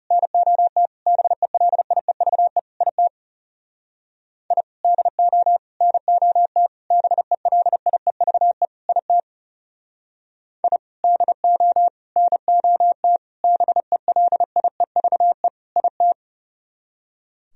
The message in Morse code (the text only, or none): NOT BELIEVE IT I DO NOT BELIEVE IT I DO NOT BELIEVE IT